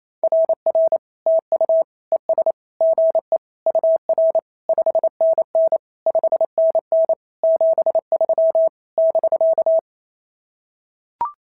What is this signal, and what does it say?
Morse code, heard as a keyed tone, about 27 wpm, 690 Hz. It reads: RR TU ES GE UR 5NN 5NN 73 <BK>
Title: SOTA QSO Practice 28wpm